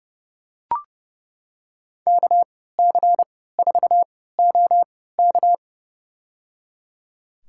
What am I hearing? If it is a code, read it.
Morse code, heard as a keyed tone, about 30 wpm, spaced as for 17 wpm, 700 Hz. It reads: KC4OK
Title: USA Call Signs - 5 Characters - 3x Character Spacing - 30wpm